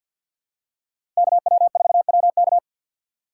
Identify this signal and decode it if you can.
Morse code, heard as a keyed tone, about 53 wpm, 700 Hz. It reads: KW4WK
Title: USA Call Signs - 6x Character Spacing - 50wpm